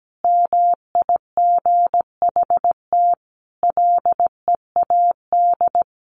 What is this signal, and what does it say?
Morse code, heard as a keyed tone, about 17 wpm, 700 Hz. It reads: MIGHT LEAD
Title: Sentences from Top 1200 Words 17wpm